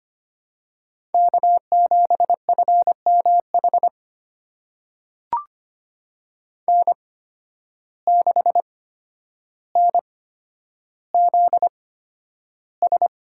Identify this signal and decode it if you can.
Morse code, heard as a keyed tone, about 25 wpm, spaced as for 11 wpm, 700 Hz. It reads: K7FMH N6NZS
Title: USA Call Signs - 5 Characters - 8x Character Spacing - 25wpm